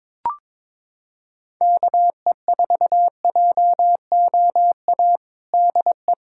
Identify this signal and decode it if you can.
Morse code, heard as a keyed tone, about 22 wpm, 700 Hz. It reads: KE4JOA DE
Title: USA Callsign DE Callsign - 6 Characters - 22wpm